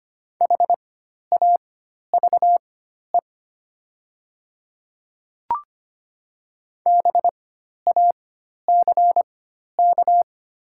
Morse code, heard as a keyed tone, about 25 wpm, spaced as for 12 wpm, 700 Hz. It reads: HAVE BACK
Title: 4x Character Spacing - Top 200 Words - 25wpm